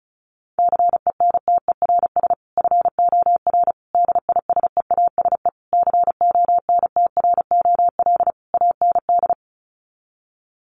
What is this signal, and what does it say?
Morse code, heard as a keyed tone, about 34 wpm, 700 Hz. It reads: CENTERS FOR DISEASE CONTROL AND
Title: Sets of 5 Words Encore2 35wpm